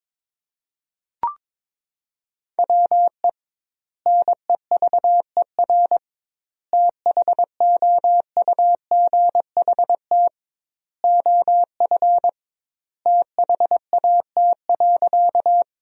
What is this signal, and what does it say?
Morse code, heard as a keyed tone, about 22 wpm, 700 Hz. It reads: WE NEVER THOUGHT OF THAT.
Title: 2x Word-Spacing - Sentences from Top 300 Words - 22wpm